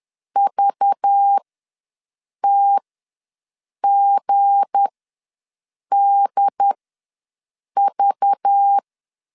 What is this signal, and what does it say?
Morse code, heard as a keyed tone, about 10 wpm, spaced as for 6 wpm, 800 Hz. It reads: VTGDV
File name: LEZIONI / 7 GF9Z / 7 G ws 10wpm 3843